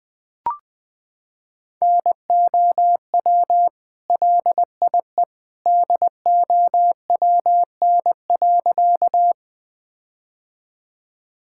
Morse code, heard as a keyed tone, about 19 wpm, 700 Hz. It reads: NOW LIE DOWN.
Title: Sentences from Top 1600 Words 20wpm